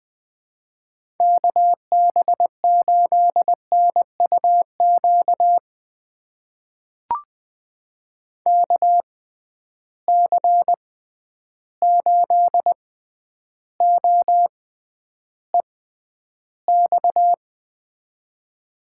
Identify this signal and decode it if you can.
Morse code, heard as a keyed tone, about 20 wpm, spaced as for 10 wpm, 700 Hz. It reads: KB8NUQ KC8OEX
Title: USA Call Signs - 6 Characters - 6x Character Spacing - 20wpm